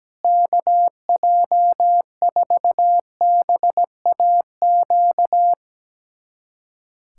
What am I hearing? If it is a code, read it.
Morse code, heard as a keyed tone, about 17 wpm, 700 Hz. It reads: KJ4BAQ
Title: USA Callsign DE Callsign - 6 Characters - 17wpm